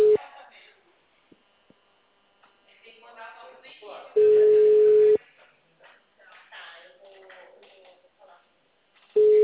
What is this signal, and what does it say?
Ringback tone (Europe, 425 Hz), with speech.
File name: out- -2016- - - .30687